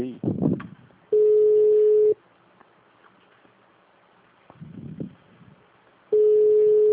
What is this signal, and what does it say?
Ringback tone (Europe, 425 Hz), with speech.
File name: out- -2011- - - .186